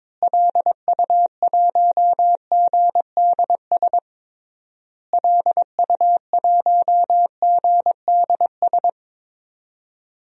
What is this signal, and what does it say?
Morse code, heard as a keyed tone, about 22 wpm, 700 Hz. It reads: LU1GDS LU1GDS
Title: DXCC-100 Call Signs - 6 Characters - Repeated 3x - 22wpm